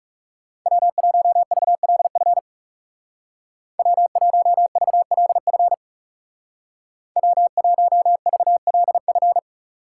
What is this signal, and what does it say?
Morse code, heard as a keyed tone, about 37 wpm, 700 Hz. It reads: W1VLF W1VLF W1VLF
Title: Speed Racing - USA Call Signs 30wpm